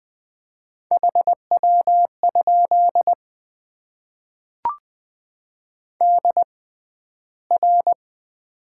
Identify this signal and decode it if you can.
Morse code, heard as a keyed tone, about 20 wpm, spaced as for 12 wpm, 700 Hz. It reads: HW? DR
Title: Top 90 QSO Elements - 6x Character Spacing - 20wpm